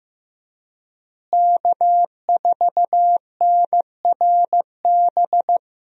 Morse code, heard as a keyed tone, about 16 wpm, 700 Hz. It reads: K4NRB
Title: USA Call Signs - 5 Characters - Repeated 3x - 15wpm